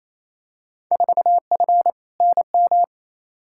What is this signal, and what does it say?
Morse code, heard as a keyed tone, about 28 wpm, 700 Hz. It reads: 4F NM
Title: Field Day Exchanges Encore - 28wpm